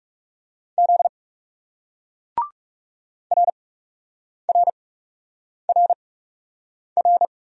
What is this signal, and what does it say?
Morse code, heard as a keyed tone, about 37 wpm, spaced as for 8 wpm, 700 Hz. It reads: Z RRRR
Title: Speed-Racing Single Letters 30wpm